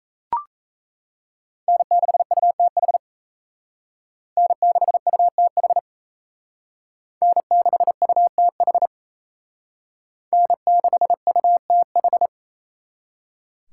Morse code, heard as a keyed tone, about 35 wpm, 700 Hz. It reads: N6UTH N6UTH N6UTH N6UTH
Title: Speed Racing - USA 1x3 Call Signs 28wpm